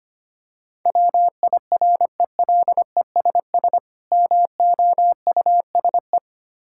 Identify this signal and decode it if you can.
Morse code, heard as a keyed tone, about 25 wpm, 700 Hz. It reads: WIRELESS MOUSE
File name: practice / 25wpm / Difficult Words with Hints - 25wpm